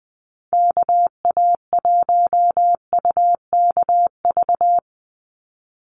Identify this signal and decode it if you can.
Morse code, heard as a keyed tone, about 20 wpm, 700 Hz. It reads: KA1UKV
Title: USA Call Signs 20wpm